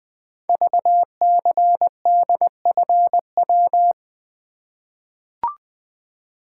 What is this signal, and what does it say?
Morse code, heard as a keyed tone, about 20 wpm, 700 Hz. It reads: VCDFW
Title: Code Groups - 5 Letters - 20wpm